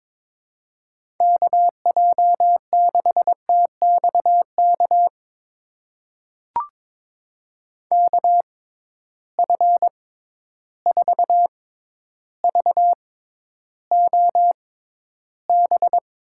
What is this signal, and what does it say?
Morse code, heard as a keyed tone, about 22 wpm, spaced as for 11 wpm, 700 Hz. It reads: KJ6TXK KF4VOB
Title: USA Call Signs - 6 Characters - 6x Character Spacing - 22wpm